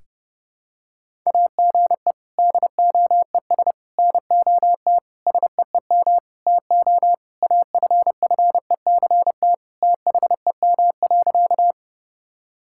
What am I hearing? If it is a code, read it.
Morse code, heard as a keyed tone, about 30 wpm, 700 Hz. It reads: AGE DOES NOT SEEM TO AFFECT THEM.
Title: Sentences from Top 1000 Words - Review Part 6 - 30wpm